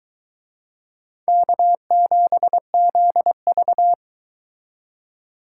Morse code, heard as a keyed tone, about 23 wpm, 700 Hz. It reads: K7ZV